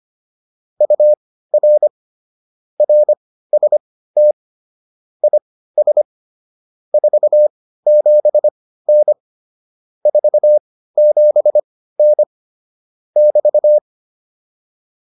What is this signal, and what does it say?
Morse code, heard as a keyed tone, about 25 wpm, spaced as for 15 wpm, 600 Hz. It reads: UR RST IS 47N 47N <BT>